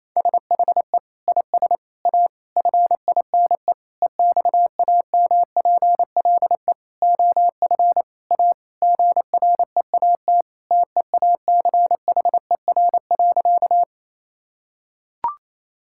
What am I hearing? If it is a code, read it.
Morse code, heard as a keyed tone, about 28 wpm, 700 Hz. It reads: SHE IS A FINE EXAMPLE OF A GREAT TEACHER.
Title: Sentences from Top 1000 Words - Review Part 6 - 28wpm